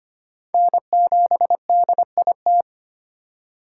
Morse code, heard as a keyed tone, about 25 wpm, 700 Hz. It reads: N7DIT
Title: USA Call Signs - 5 Characters - 4x Character Spacing - 25wpm